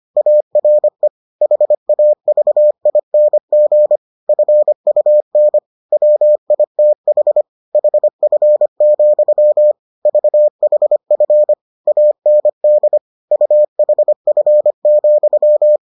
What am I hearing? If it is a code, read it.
Morse code, heard as a keyed tone, about 25 wpm, 600 Hz. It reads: ARE HAVING FUN WITH HF, VHF AND UHF,